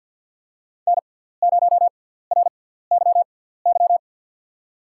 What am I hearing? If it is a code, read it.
Morse code, heard as a keyed tone, about 52 wpm, spaced as for 17 wpm, 700 Hz. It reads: N0RYY
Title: USA Call Signs - 5 Characters - 6x Character Spacing - 50wpm